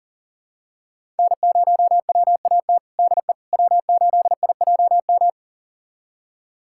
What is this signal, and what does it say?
Morse code, heard as a keyed tone, about 37 wpm, 700 Hz. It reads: N0WAT DE W8IJM
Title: USA Callsign DE Callsign - 5 Characters - 40wpm